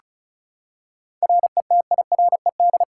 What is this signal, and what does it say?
Morse code, heard as a keyed tone, about 36 wpm, 700 Hz. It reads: RETIRED
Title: K1USN SST QSO Practice 35wpm